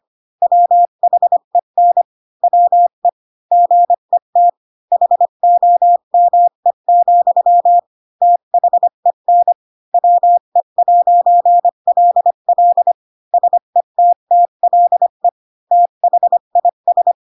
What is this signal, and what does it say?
Morse code, heard as a keyed tone, about 25 wpm, 700 Hz. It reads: WHEN WE GET HOME, THEN WE'LL SETTLE THIS
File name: Pinocchio / Pinocchio ch3 20 25 ik1hge